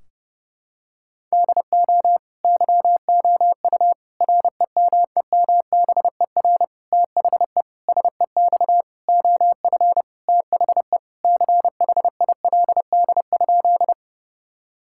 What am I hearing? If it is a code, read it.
Morse code, heard as a keyed tone, about 30 wpm, 700 Hz. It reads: DO YOU REMEMBER THE SEX OF THE CHILD?
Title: Sentences from Top 1600 Words 30wpm